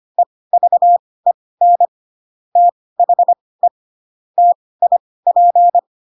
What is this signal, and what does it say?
Morse code, heard as a keyed tone, about 25 wpm, spaced as for 18 wpm, 700 Hz. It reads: EVEN THE TIP
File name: Pinocchio / Pinocchio ch1 12 25 ik1hge